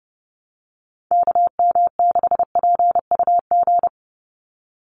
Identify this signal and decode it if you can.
Morse code, heard as a keyed tone, about 30 wpm, 700 Hz. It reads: KM6PUG
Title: USA Call Signs 30wpm